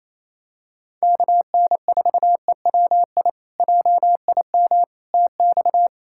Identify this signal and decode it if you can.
Morse code, heard as a keyed tone, about 28 wpm, 700 Hz. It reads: KN4EWI JIM TX
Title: K1USN SST QSO Practice - 28wpm